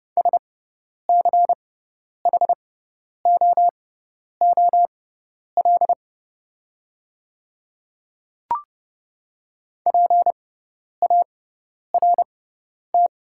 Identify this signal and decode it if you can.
Morse code, heard as a keyed tone, about 30 wpm, spaced as for 10 wpm, 700 Hz. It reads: SCHOOL PART
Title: 6x Character Spacing - Top 300 Words - 30wpm